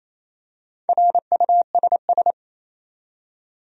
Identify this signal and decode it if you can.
Morse code, heard as a keyed tone, about 27 wpm, 700 Hz. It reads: RUSS
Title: USA Common Names - Max 4 Letters - 28wpm